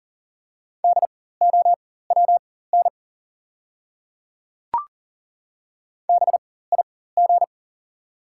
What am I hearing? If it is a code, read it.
Morse code, heard as a keyed tone, about 40 wpm, spaced as for 19 wpm, 690 Hz. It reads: DOWN BIG
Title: 4x Character Spacing - Top 200 Words - 40wpm